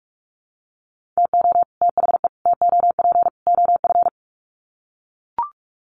Morse code, heard as a keyed tone, about 44 wpm, 700 Hz. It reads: TO THE TOP OF
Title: Sets of 4 Words 45wpm